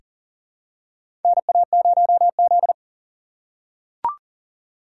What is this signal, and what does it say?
Morse code, heard as a keyed tone, about 40 wpm, 700 Hz. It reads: NA0Z
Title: USA Call Signs - 4 Characters - 6x Character Spacing - 40wpm